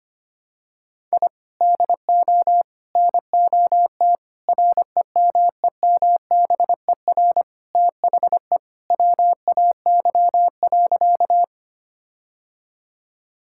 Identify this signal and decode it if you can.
Morse code, heard as a keyed tone, about 25 wpm, 700 Hz. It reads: I DO NOT REMEMBER THE WAY.
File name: practice / 25wpm / Sentences with 4-Word Phrases - 25wpm